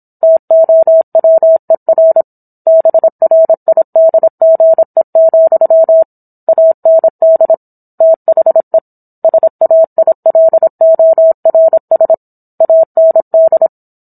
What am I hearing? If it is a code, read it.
Morse code, heard as a keyed tone, about 27 wpm, 650 Hz. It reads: TOWER BRIDGE, AND THE SAILORS AND
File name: Book One / 17-Chapter 17 - 26 WPM